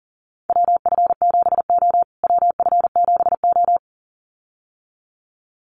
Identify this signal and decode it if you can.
Morse code, heard as a keyed tone, about 40 wpm, 700 Hz. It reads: WF7O WF7O
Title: USA Call Signs Repeated 2x 40wpm